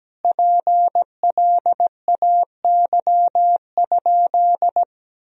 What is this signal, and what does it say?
Morse code, heard as a keyed tone, about 17 wpm, 700 Hz. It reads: PLAY?
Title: Sentences from Top 400 Words and Abbreviations 17wpm